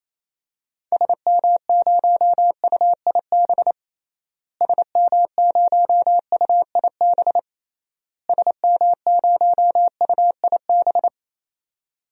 Morse code, heard as a keyed tone, about 27 wpm, 700 Hz. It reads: SM0UIB SM0UIB SM0UIB
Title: DXCC-100 Call Signs - 6 Characters - Repeated 3x - 28wpm